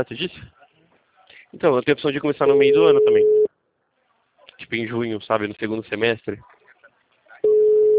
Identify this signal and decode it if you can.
Ringback tone (Europe, 425 Hz), with speech.